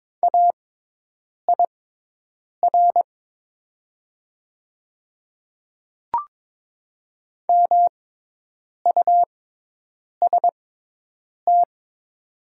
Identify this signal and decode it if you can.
Morse code, heard as a keyed tone, about 22 wpm, spaced as for 8 wpm, 690 Hz. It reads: AIR MUST
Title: 6x Character Spacing - Top 200 Words - 22wpm